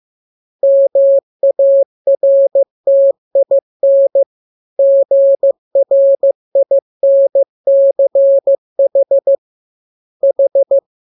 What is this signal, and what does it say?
Morse code, heard as a keyed tone, about 15 wpm, 550 Hz. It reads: MARTIN GRINCH H